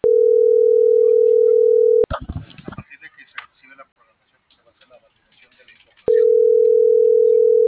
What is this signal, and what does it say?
Ringback tone (North America), 440 + 480 Hz, with speech.
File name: CENTRAL TERRESTRE